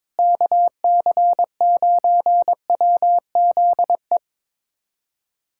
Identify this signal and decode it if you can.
Morse code, heard as a keyed tone, about 22 wpm, 700 Hz. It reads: KC9WZE
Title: USA Callsign DE Callsign - 6 Characters - 22wpm